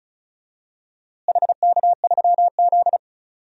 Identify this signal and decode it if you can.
Morse code, heard as a keyed tone, about 34 wpm, 700 Hz. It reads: HK3Z
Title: DXCC-100 Call Signs - 4 Characters - Repeated 2x - 35wpm